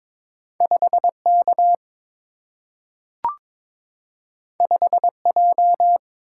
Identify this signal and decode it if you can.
Morse code, heard as a keyed tone, about 22 wpm, 700 Hz. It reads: 5 K 5 J
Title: Single-Letter Single-Number 22wpm